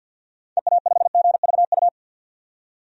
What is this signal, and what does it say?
Morse code, heard as a keyed tone, about 50 wpm, 700 Hz. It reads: EA5GVU
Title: DXCC-100 Call Signs - 6 Characters - 50wpm